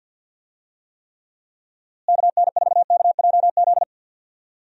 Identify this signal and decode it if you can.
Morse code, heard as a keyed tone, about 50 wpm, 700 Hz. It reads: KN4KJZ
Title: USA Call Signs - 6 Characters - Repeated 2x - 50wpm